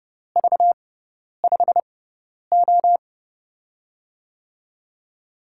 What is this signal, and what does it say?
Morse code, heard as a keyed tone, about 30 wpm, spaced as for 10 wpm, 700 Hz. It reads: V5O
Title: USA Call Signs - 6x Character Spacing - 30wpm